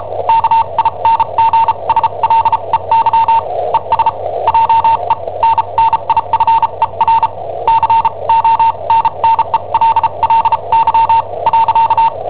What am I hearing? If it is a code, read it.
Morse code, heard as a keyed tone, about 34 wpm, spaced as for 28 wpm, 910 Hz. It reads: KINGSLEY ES JENNIFER CONNELLY .
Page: Cassetta esercizi preparata da I7OHP per IW5ELP